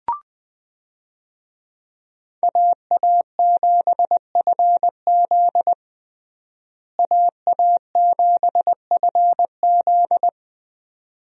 Morse code, heard as a keyed tone, about 20 wpm, 700 Hz. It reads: AA7FZ AA7FZ
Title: USA Call Signs - 5 Characters - Repeated 2x - 20wpm